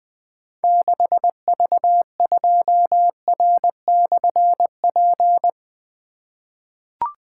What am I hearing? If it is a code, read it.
Morse code, heard as a keyed tone, about 20 wpm, 700 Hz. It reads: 6V2R/P
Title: Difficult Call Signs 20wpm